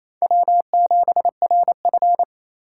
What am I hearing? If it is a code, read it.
Morse code, heard as a keyed tone, about 28 wpm, 700 Hz. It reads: W7RF
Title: USA Call Signs - 3x Character Spacing - 28wpm